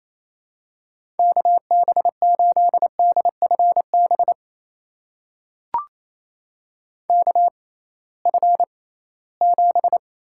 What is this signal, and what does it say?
Morse code, heard as a keyed tone, about 28 wpm, spaced as for 18 wpm, 700 Hz. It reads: KB8DFB KF7
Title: USA Call Signs - 6 Characters - 6x Character Spacing - 28wpm